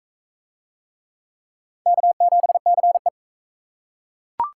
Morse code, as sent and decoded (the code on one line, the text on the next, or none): -.- --... -.-. .
K7CE